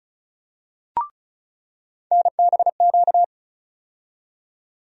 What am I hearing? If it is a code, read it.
Morse code, heard as a keyed tone, about 34 wpm, 700 Hz. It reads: NBQ
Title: USA Call Sign Postfixes - 3 Letters 35wpm